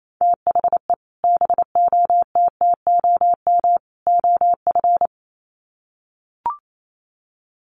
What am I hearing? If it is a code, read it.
Morse code, heard as a keyed tone, about 28 wpm, 700 Hz. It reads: THE BOTTOM OF
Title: Sets of 3 Words 28wpm